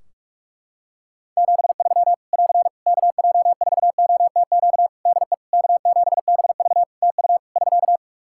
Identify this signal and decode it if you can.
Morse code, heard as a keyed tone, about 46 wpm, 700 Hz. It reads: 73 <AR> KJ4OTQ DE K7BV TU <SK>
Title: K1USN SST QSO Practice 45wpm